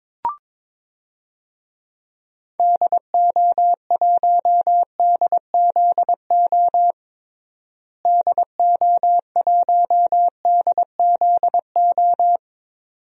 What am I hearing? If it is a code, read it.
Morse code, heard as a keyed tone, about 22 wpm, 700 Hz. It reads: DO1DZO DO1DZO
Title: DXCC-100 Call Signs - 6 Characters - Repeated 2x - 22wpm